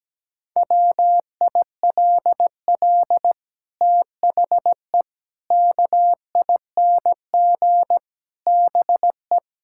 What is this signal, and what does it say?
Morse code, heard as a keyed tone, about 17 wpm, 700 Hz. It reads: WILL THE KING BE
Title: Sentences from Top 1000 Words - Review Part 6 - 17wpm